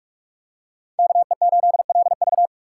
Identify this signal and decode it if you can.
Morse code, heard as a keyed tone, about 46 wpm, 700 Hz. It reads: KE8LV
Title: USA Call Signs - 3x Character Spacing - 45wpm